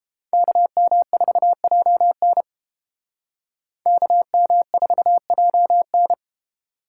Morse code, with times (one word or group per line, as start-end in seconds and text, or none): 0.3-2.4 KM4JN
3.9-6.1 KM4JN